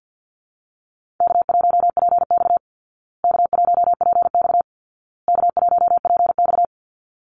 Morse code, heard as a keyed tone, about 49 wpm, 700 Hz. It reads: K1PX K1PX K1PX
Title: Easy Call Signs Repeated 3x - 50wpm